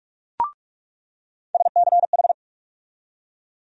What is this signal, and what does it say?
Morse code, heard as a keyed tone, about 42 wpm, 700 Hz. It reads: SCH
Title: USA Call Sign Postfixes - 3 Letters 45wpm